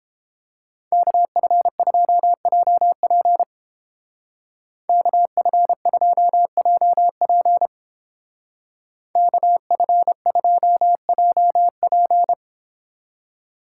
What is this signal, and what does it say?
Morse code, heard as a keyed tone, about 29 wpm, 700 Hz. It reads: KF2JP KF2JP KF2JP